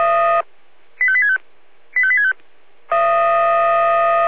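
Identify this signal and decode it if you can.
Five-tone selective call (ZVEI), heard as quick runs of five tones, tones of about 70 ms. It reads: 85865 85865
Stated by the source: ZVEI five-tone selective call 85880, received on SDR.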